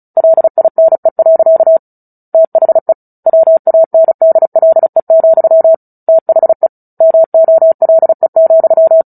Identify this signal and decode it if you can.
Morse code, heard as a keyed tone, about 34 wpm, 650 Hz. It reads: LINE. THE WANDLE, THE MOLE,
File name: Book Two / 09-Chapter 9 - 35 WPM